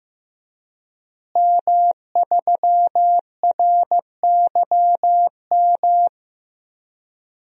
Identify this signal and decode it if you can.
Morse code, heard as a keyed tone, about 15 wpm, 700 Hz. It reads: M3RYM
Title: DXCC-100 Call Signs - 5 Characters - Repeated 2x - 15wpm